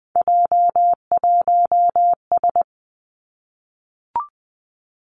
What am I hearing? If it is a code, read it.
Morse code, heard as a keyed tone, about 20 wpm, 700 Hz. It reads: J1S
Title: Three Challenging Letter-Number Combinations 20wpm